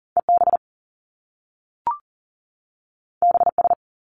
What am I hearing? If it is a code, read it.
Morse code, heard as a keyed tone, about 40 wpm, 700 Hz. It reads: EB BS
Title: Top Two Letter Combinations in Words 40wpm